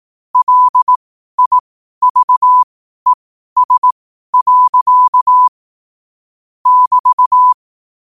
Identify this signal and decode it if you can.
Morse code, heard as a keyed tone, about 18 wpm, spaced as for 12 wpm, 1000 Hz. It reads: LIVES. <BT>
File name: qotdfiles / 12wpm